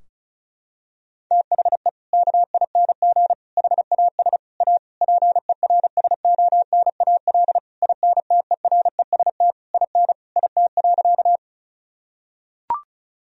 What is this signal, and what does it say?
Morse code, heard as a keyed tone, about 34 wpm, 700 Hz. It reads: THE KING HAS A PERSONAL INTEREST IN IT.
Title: Sentences from Top 1000 Words - Review Part 4 - 35wpm